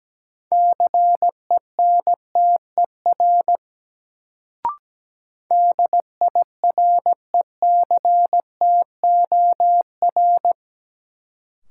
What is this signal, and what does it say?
Morse code, heard as a keyed tone, about 17 wpm, 700 Hz. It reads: CENTER DIRECTOR
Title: Sets of 3 Words - New Word Drill - 17wpm